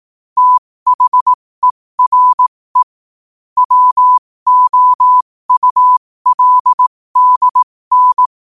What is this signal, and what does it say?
Morse code, heard as a keyed tone, about 18 wpm, spaced as for 15 wpm, 1000 Hz. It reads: THERE WOULDN
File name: qotdfiles / 15wpm